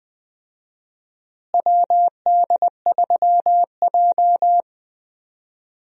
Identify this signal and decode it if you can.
Morse code, heard as a keyed tone, about 20 wpm, 700 Hz. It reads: WD3J
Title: USA Call Signs - 4 Characters - Repeated 2x - 20wpm